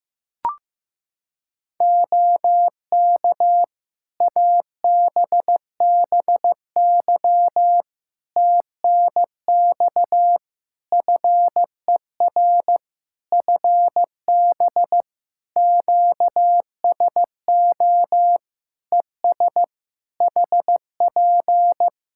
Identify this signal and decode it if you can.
Morse code, heard as a keyed tone, about 15 wpm, 700 Hz. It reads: OK ABBY TNX FER FB QSO ES HP
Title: Practice QSO - Exchange 4 - Ending - 15wpm